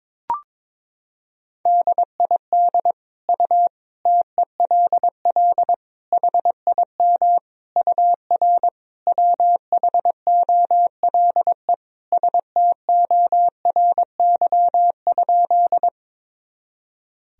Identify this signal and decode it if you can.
Morse code, heard as a keyed tone, about 22 wpm, 700 Hz. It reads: DID U TELL HIM UR WHOLE STORY?
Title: Sentences from Top 400 Words and Abbreviations 22wpm